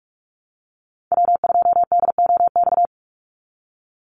Morse code, heard as a keyed tone, about 44 wpm, 700 Hz. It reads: W2DOX
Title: USA Call Signs 45wpm